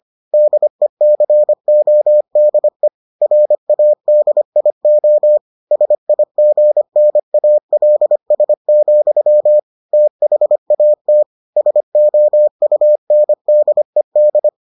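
Morse code, heard as a keyed tone, about 25 wpm, 600 Hz. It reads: DECODE RADIO SIGNALS, THAT SOUNDED
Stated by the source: Morse edition of the Foundations of Amateur Radio episodes.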